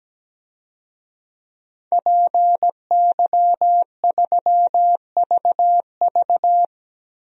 Morse code, heard as a keyed tone, about 17 wpm, 700 Hz. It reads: PY3VV
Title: DXCC-100 Call Signs - 5 Characters - Repeated 2x - 17wpm